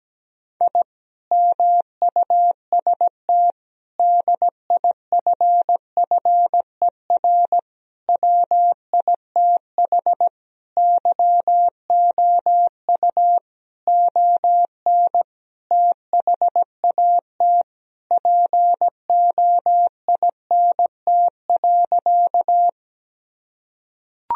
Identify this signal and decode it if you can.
Morse code, heard as a keyed tone, about 17 wpm, 700 Hz. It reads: I MUST DIFFER WITH YOU ON THAT POINT.
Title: Sentences from Top 1000 Words - Review Part 1 - 17wpm